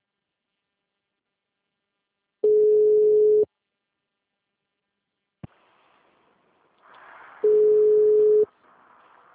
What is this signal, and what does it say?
Ringback tone (Europe, 425 Hz).